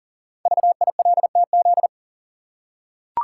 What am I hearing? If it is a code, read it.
Morse code, heard as a keyed tone, about 44 wpm, 700 Hz. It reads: VILTZ
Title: Code Groups - 5 Letters - 40wpm